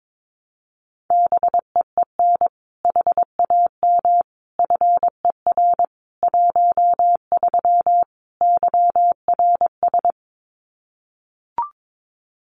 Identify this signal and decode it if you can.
Morse code, heard as a keyed tone, about 22 wpm, 700 Hz. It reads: BEEN HAM FER 13 YRS
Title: Common QSO Phrases 22wpm